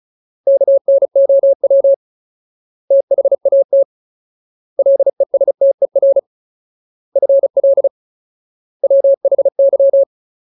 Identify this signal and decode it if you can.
Morse code, heard as a keyed tone, about 34 wpm, 550 Hz. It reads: KNOW THAT LESTER FL WHY
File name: Ultimate Word List / 35WPM Ultimate